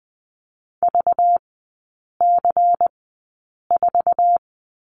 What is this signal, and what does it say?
Morse code, heard as a keyed tone, about 20 wpm, spaced as for 8 wpm, 700 Hz. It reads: VC4
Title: 205 Lesson 33e - Letter j - Three Characters 20wpm